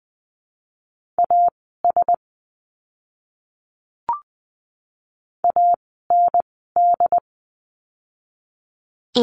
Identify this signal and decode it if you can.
Morse code, heard as a keyed tone, about 20 wpm, spaced as for 15 wpm, 700 Hz. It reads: AS AND
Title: 192 Lesson 31f - Letter v - Words 20wpm